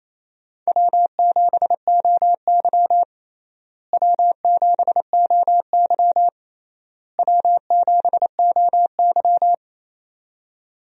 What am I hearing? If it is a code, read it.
Morse code, heard as a keyed tone, about 28 wpm, 700 Hz. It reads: W7OY W7OY W7OY
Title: USA Call Signs - 4 Characters - Repeated 3x - 28wpm